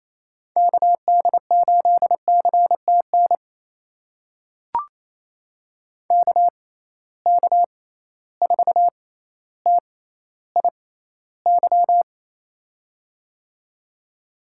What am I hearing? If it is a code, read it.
Morse code, heard as a keyed tone, about 28 wpm, spaced as for 14 wpm, 700 Hz. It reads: KD8CTN KK4TIY
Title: USA Call Signs - 6 Characters - 6x Character Spacing - 28wpm